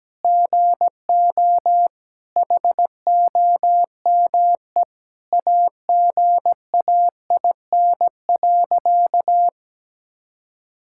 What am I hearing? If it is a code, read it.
Morse code, heard as a keyed tone, about 17 wpm, 700 Hz. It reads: GO HOME AGAIN.
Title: Sentences from Top 1000 Words - Review Part 6 - 17wpm